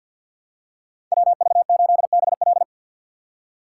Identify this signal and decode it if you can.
Morse code, heard as a keyed tone, about 53 wpm, 700 Hz. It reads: WV8BL
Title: USA Call Signs - 5 Characters - 6x Character Spacing - 50wpm